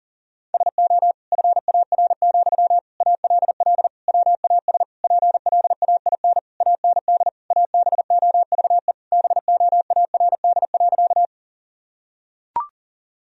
Morse code, heard as a keyed tone, about 37 wpm, 700 Hz. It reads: SO FAR, ALL WAS PLAIN AND ABOVE BOARD.
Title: Sentences from Top 1000 Words - Review Part 4 - 40wpm